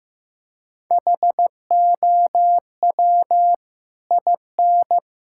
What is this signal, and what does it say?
Morse code, heard as a keyed tone, about 15 wpm, 700 Hz. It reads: HOW IN
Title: Sentences from Top 1000 Words - Review Part 5 - 15wpm